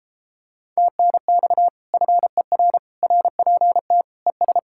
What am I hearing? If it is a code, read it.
Morse code, heard as a keyed tone, about 33 wpm, 700 Hz. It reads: TNX FER RPT ES